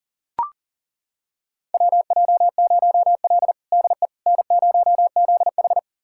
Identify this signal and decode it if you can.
Morse code, heard as a keyed tone, about 40 wpm, 700 Hz. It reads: WJ0L DE N0ZH
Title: USA Callsign DE Callsign - 4 Characters - 40wpm